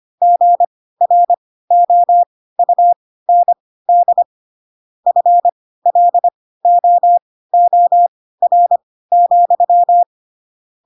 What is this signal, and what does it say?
Morse code, heard as a keyed tone, about 25 wpm, spaced as for 16 wpm, 700 Hz. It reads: GROUND FLOOR,
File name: Pinocchio / Pinocchio ch3 10 25 ik1hge